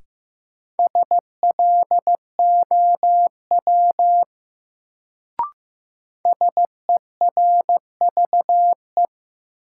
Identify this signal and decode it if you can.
Morse code, heard as a keyed tone, about 15 wpm, 700 Hz. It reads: SLOW SERVE
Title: Sentences from Top 400 Words - New Word Drill - 15wpm